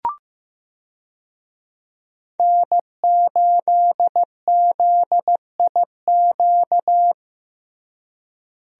Morse code, heard as a keyed tone, about 15 wpm, 700 Hz. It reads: N8ZIQ
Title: USA Call Signs - 5 Characters - Repeated 2x - 15wpm